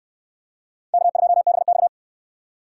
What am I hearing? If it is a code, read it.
Morse code, heard as a keyed tone, about 69 wpm, 700 Hz. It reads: K2BX